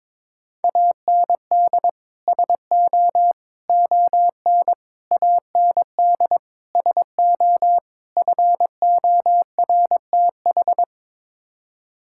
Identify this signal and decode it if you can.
Morse code, heard as a keyed tone, about 22 wpm, 700 Hz. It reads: AND SO ON AND SO FORTH
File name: practice / 22wpm / Sets of 6 Words - 22wpm